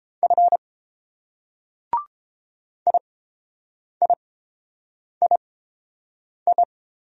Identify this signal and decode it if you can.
Morse code, heard as a keyed tone, about 29 wpm, spaced as for 7 wpm, 690 Hz. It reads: F IIII